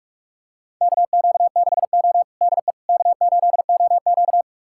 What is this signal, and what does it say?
Morse code, heard as a keyed tone, about 46 wpm, 700 Hz. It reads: KQ6O DE K8OQ
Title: USA Callsign DE Callsign - 4 Characters - 45wpm